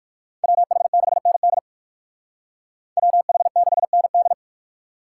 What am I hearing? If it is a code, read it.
Morse code, heard as a keyed tone, about 58 wpm, 700 Hz. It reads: WH6ND WH6ND